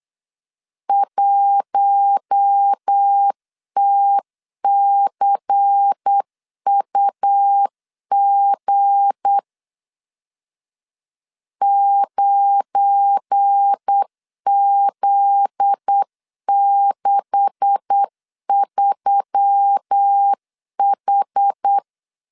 Morse code, heard as a keyed tone, about 8 wpm, 800 Hz. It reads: 1TCUG 9Z63H
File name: LEZIONI / 9 286 8wpm 3425